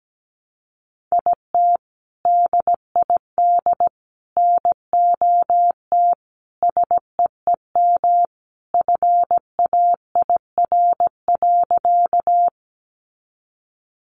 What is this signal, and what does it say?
Morse code, heard as a keyed tone, about 17 wpm, 700 Hz. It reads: IT DID NOT SEEM FAIR.
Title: Sentences from Top 1300 Words 17wpm